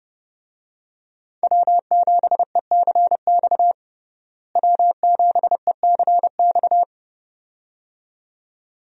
Morse code, heard as a keyed tone, about 30 wpm, 700 Hz. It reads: W7ECX W7ECX
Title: USA Call Signs - 5 Characters - Repeated 2x - 30wpm